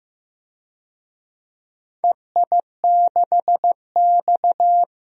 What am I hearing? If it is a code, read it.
Morse code, heard as a keyed tone, about 15 wpm, 700 Hz. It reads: EI6X
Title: DXCC-100 Call Signs - 4 Characters - Repeated 3x - 15wpm